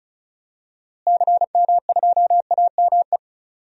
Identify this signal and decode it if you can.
Morse code, heard as a keyed tone, about 36 wpm, 700 Hz. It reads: CM2AME
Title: DXCC-100 Call Signs - 6 Characters - Repeated 2x - 35wpm